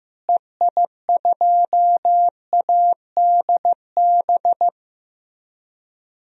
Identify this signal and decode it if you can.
Morse code, heard as a keyed tone, about 16 wpm, 700 Hz. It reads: EI2ADB
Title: DXCC-100 Call Signs - 6 Characters - Repeated 3x - 15wpm